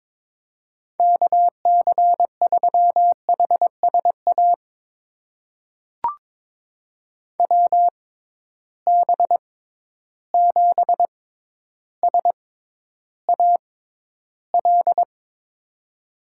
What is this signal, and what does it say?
Morse code, heard as a keyed tone, about 22 wpm, spaced as for 11 wpm, 700 Hz. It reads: KC3HSA WB7SAL
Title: USA Call Signs - 6 Characters - 6x Character Spacing - 22wpm